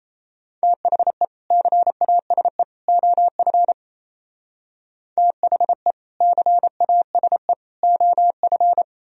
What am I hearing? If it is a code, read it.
Morse code, heard as a keyed tone, about 35 wpm, 700 Hz. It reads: THE CASE OF THE CASE OF
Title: Speed-Racing - Sets of 3 Words 28wpm